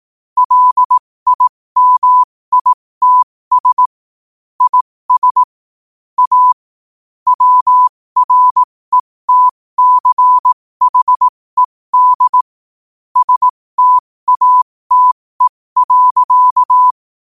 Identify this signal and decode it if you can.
Morse code, heard as a keyed tone, about 18 wpm, spaced as for 15 wpm, 1000 Hz. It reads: LIMITS IS A WRETCHED STATE.
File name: qotdfiles / 15wpm